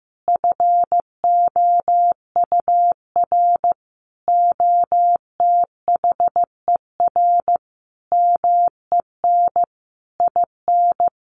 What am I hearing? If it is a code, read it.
Morse code, heard as a keyed tone, about 15 wpm, 700 Hz. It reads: FOUR OTHER MEN IN
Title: A Sound Of Thunder 15wpm